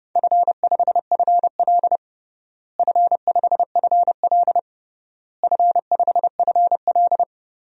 Morse code, heard as a keyed tone, about 30 wpm, 700 Hz. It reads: F5FL F5FL F5FL
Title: DXCC-100 Call Signs - 4 Characters - Repeated 3x - 30wpm